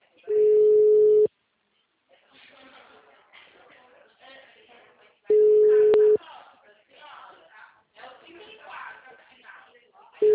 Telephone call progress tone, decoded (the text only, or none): ringback tone